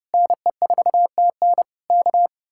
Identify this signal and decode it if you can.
Morse code, heard as a keyed tone, about 28 wpm, 700 Hz. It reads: NE4TN K
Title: Speed-Racing Common QSO Phrases - 30wpm